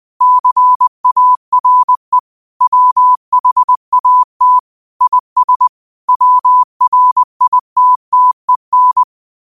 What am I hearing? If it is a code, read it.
Morse code, heard as a keyed tone, about 20 wpm, 1000 Hz. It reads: CARE WHAT IS WRITTEN
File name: 20wpm